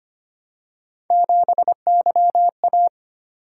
Morse code, heard as a keyed tone, about 25 wpm, 700 Hz. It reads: 7YA
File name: Three Letter-Number Combinations - 25wpm